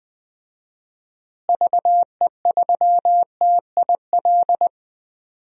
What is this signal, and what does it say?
Morse code, heard as a keyed tone, about 20 wpm, 700 Hz. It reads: VE3TIL
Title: DXCC-100 Call Signs - 6 Characters - Repeated 3x - 20wpm